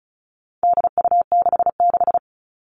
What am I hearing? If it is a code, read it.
Morse code, heard as a keyed tone, about 34 wpm, 700 Hz. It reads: DU66
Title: All Cut Numbers - 4-Digit 35wpm